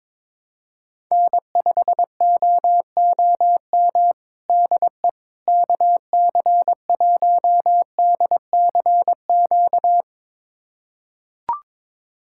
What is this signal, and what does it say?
Morse code, heard as a keyed tone, about 22 wpm, 700 Hz. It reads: N5OOM DE KC1DCQ
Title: USA Callsign DE Callsign 22wpm